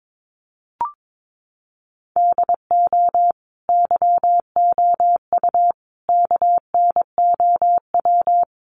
Morse code, heard as a keyed tone, about 22 wpm, 700 Hz. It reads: DO YOU KNOW
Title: Sentences from Top 100 Words 22wpm